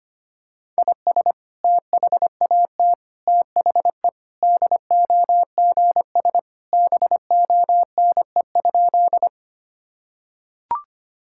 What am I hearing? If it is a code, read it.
Morse code, heard as a keyed tone, about 25 wpm, 700 Hz. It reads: IS THAT THE DOGS BONE?
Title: Sentences from Top 1600 Words 25wpm